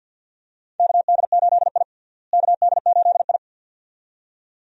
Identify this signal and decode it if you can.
Morse code, heard as a keyed tone, about 50 wpm, 700 Hz. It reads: KD8I KD8I
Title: USA Call Signs - 4 Characters - Repeated 2x - 50wpm